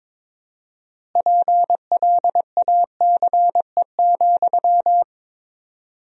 Morse code, heard as a keyed tone, about 22 wpm, 700 Hz. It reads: PLACE,
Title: 4x Word-Spacing - Sentences from Top 400 Words - 22wpm